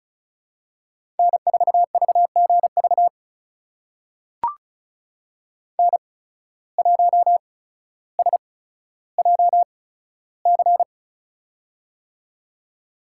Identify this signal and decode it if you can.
Morse code, heard as a keyed tone, about 34 wpm, spaced as for 15 wpm, 700 Hz. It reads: N4VGV N1SJC
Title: USA Call Signs - 5 Characters - 8x Character Spacing - 35wpm